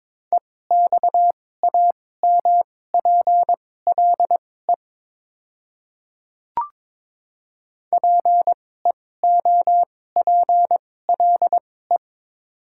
Morse code, heard as a keyed tone, about 22 wpm, spaced as for 16 wpm, 700 Hz. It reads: EXAMPLE PEOPLE
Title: 2x Character Spacing - Top 300 Words - 22wpm